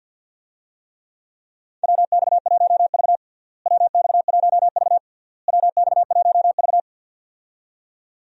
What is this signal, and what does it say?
Morse code, heard as a keyed tone, about 50 wpm, 700 Hz. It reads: WX1V WX1V WX1V
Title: USA Call Signs - 4 Characters - Repeated 3x - 50wpm